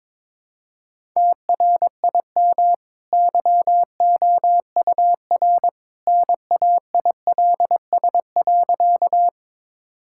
Morse code, heard as a keyed tone, about 22 wpm, 700 Hz. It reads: TRIM YOUR NAILS.
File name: practice / 22wpm / Easy Sentences with Hints - 22wpm